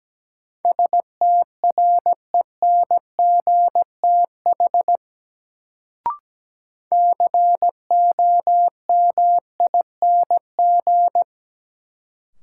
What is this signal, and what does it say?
Morse code, heard as a keyed tone, about 17 wpm, 700 Hz. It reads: STRENGTH COMING
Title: Sentences from Top 1300 Words - New Word Drill Alt - 17wpm